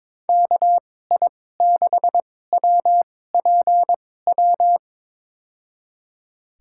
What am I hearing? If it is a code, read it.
Morse code, heard as a keyed tone, about 22 wpm, spaced as for 16 wpm, 700 Hz. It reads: KI6WPW
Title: USA Call Signs - 6 Characters - 2x Character Spacing - 22wpm